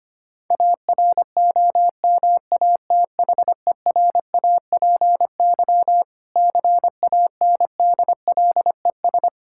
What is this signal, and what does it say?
Morse code, heard as a keyed tone, about 25 wpm, 700 Hz. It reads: AROMATHERAPY CANDLES
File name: practice / 25wpm / Difficult Words with Hints - 25wpm